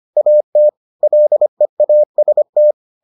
Morse code, heard as a keyed tone, about 25 wpm, 600 Hz. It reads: AT LEAST